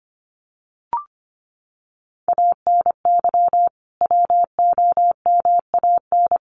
Morse code, heard as a keyed tone, about 25 wpm, 700 Hz. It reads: ANY WOMAN
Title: Sentences from Top 1100 Words 25wpm